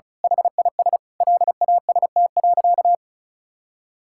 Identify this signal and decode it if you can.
Morse code, heard as a keyed tone, about 36 wpm, 700 Hz. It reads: HIS LAST.
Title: Sentences from Top 1600 Words 35wpm